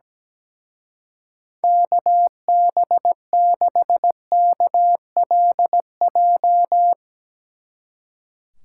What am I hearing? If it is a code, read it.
Morse code, heard as a keyed tone, about 17 wpm, 700 Hz. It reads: KB6KLJ
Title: Speed Racing - USA 2x3 Call Signs 17wpm